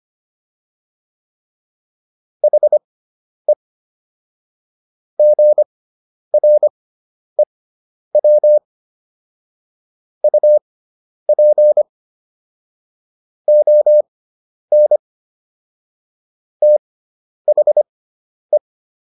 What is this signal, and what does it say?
Morse code, heard as a keyed tone, about 25 wpm, spaced as for 10 wpm, 600 Hz. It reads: HE GREW UP ON THE